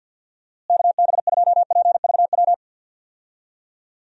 Morse code, heard as a keyed tone, about 46 wpm, 700 Hz. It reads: KB2PVW
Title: USA Call Signs - 6 Characters - Repeated 2x - 50wpm